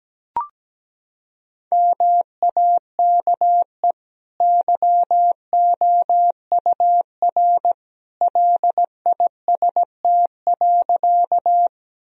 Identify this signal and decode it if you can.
Morse code, heard as a keyed tone, about 17 wpm, 700 Hz. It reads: MAKE YOUR LIST.
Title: Sentences from Top 1000 Words - Review Part 2 - 17wpm